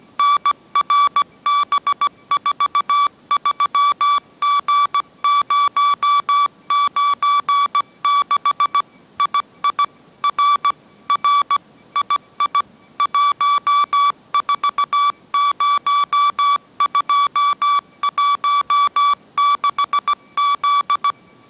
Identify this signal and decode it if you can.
Morse code, heard as a keyed tone, about 17 wpm, 1150 Hz. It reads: NRB43G096 II R R II 140216Z
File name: mrx / mrx080 18wpm